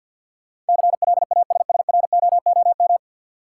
Morse code, heard as a keyed tone, about 52 wpm, 700 Hz. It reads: CLASSROOM
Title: Sentences from Top 1500 Words - New Word Drill Alt - 50wpm